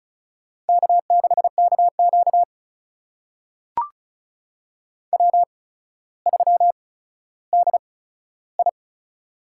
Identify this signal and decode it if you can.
Morse code, heard as a keyed tone, about 36 wpm, spaced as for 15 wpm, 700 Hz. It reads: K6KQ W3DI
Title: USA Call Signs - 8x Character Spacing - 35wpm